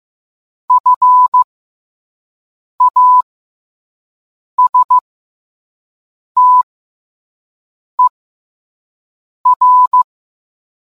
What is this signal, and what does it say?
Morse code, heard as a keyed tone, about 15 wpm, spaced as for 5 wpm, 1000 Hz. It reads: FASTER